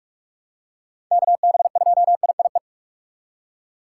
Morse code, heard as a keyed tone, about 42 wpm, 700 Hz. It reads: KB2IIE
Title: USA Call Signs - 6 Characters - 4x Character Spacing - 45wpm